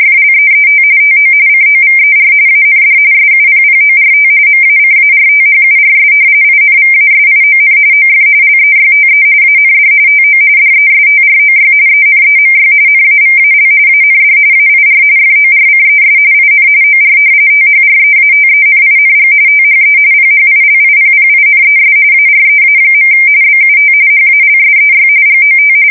RTTY, about 45.45 baud. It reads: FLEET TO COVER AND SUPPORT MACARTHURS INVASION OF LEYTE AND TO DESTROY ENEMY NAVAL AND AIR FORCES IN OR THREATENING THE PHILIPPINES AREA. BUT HALSEY GOT
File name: fsk170 / battleshipnj